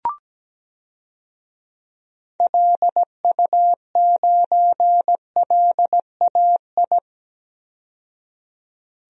Morse code, heard as a keyed tone, about 17 wpm, 700 Hz. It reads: LU9LAI
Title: DXCC-100 Call Signs - 6 Characters - 17wpm